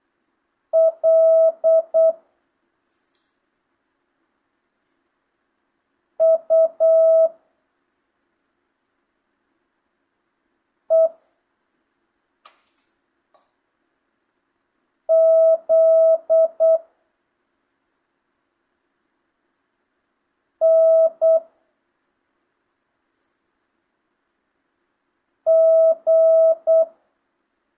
Morse code, heard as a keyed tone, about 8 wpm, 650 Hz. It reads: L U E Z N G
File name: jh8cbh / 2025cw